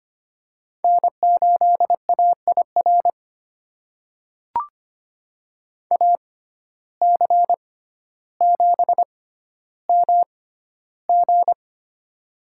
Morse code, heard as a keyed tone, about 25 wpm, spaced as for 13 wpm, 700 Hz. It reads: N8AIR AC7MG
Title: USA Call Signs - 5 Characters - 6x Character Spacing - 25wpm